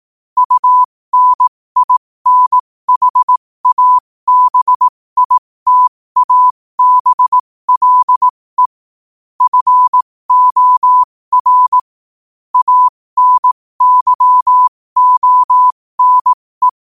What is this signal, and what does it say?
Morse code, heard as a keyed tone, about 18 wpm, spaced as for 15 wpm, 1000 Hz. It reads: UNINHABITABLE FOR ANYONE
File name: qotdfiles / 15wpm